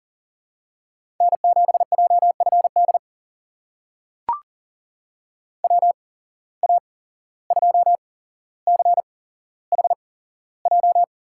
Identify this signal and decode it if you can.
Morse code, heard as a keyed tone, about 40 wpm, spaced as for 16 wpm, 700 Hz. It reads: N7JFD WA2CHJ
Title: USA Call Signs - 8x Character Spacing - 40wpm